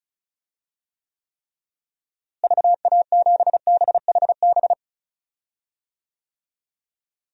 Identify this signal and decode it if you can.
Morse code, heard as a keyed tone, about 34 wpm, 700 Hz. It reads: VA7BHB
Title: DXCC-100 Call Signs - 6 Characters - Repeated 3x - 35wpm